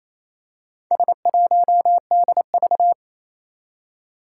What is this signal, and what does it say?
Morse code, heard as a keyed tone, about 28 wpm, 700 Hz. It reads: S1DV
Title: Four Challenging Letter-Number Combinations 28wpm